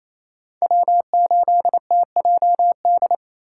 Morse code, heard as a keyed tone, about 33 wpm, 700 Hz. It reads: W8TJD
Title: Speed Racing - USA 1x3 Call Signs 28wpm